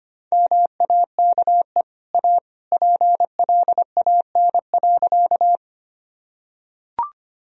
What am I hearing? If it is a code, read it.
Morse code, heard as a keyed tone, about 25 wpm, 700 Hz. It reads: MAKE A PLAN.
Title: Sentences from Top 1000 Words - Review Part 2 - 25wpm